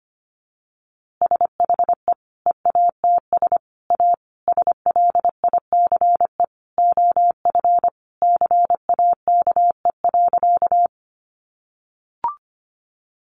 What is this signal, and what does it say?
Morse code, heard as a keyed tone, about 25 wpm, 700 Hz. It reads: SHE EATS A SLICE OF CAKE.
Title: Easy Phrases 25wpm